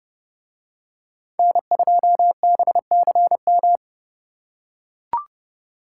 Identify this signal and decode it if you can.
Morse code, heard as a keyed tone, about 29 wpm, 700 Hz. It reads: N2BCM